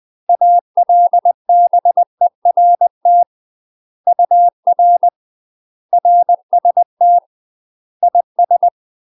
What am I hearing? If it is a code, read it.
Morse code, heard as a keyed tone, about 20 wpm, 700 Hz. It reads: ALBERT UR RST IS
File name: UK MASTER QSOs TRAINING / 20WPM / UK MASTER QSOs 0025